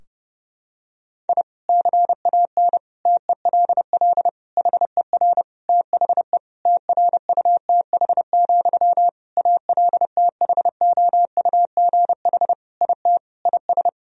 Morse code, heard as a keyed tone, about 30 wpm, 700 Hz. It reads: I CAN TELL HER THE TRUTH, ALTHOUGH IT IS
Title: Sentences from Top 1000 Words - Review Part 3 - 30wpm